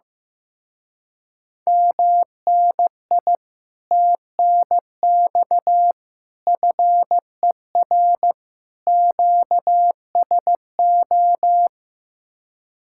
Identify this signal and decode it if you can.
Morse code, heard as a keyed tone, about 15 wpm, 700 Hz. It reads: MNI TNX FER QSO
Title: Speed-Racing Common QSO Phrases Encore - 15wpm